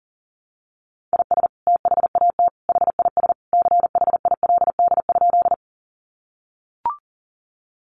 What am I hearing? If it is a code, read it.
Morse code, heard as a keyed tone, about 40 wpm, 700 Hz. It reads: IS THAT HIS CHILD?
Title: Sentences from Top 800 Words 40wpm